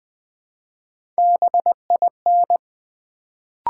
Morse code, heard as a keyed tone, about 20 wpm, 700 Hz. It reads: BIN
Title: Build A Word - 1 Letter At A Time - 3-Letter Words - 20wpm